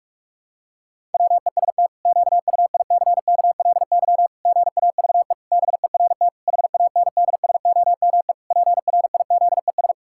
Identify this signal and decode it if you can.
Morse code, heard as a keyed tone, about 41 wpm, 700 Hz. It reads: WEST QUICKLY GAVE BERT HANDSOME PRIZES
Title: Pangram Challenge v2 - 45wpm